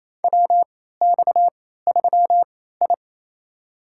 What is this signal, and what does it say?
Morse code, heard as a keyed tone, about 28 wpm, spaced as for 16 wpm, 700 Hz. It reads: WX3I
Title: USA Call Signs - 3x Character Spacing - 28wpm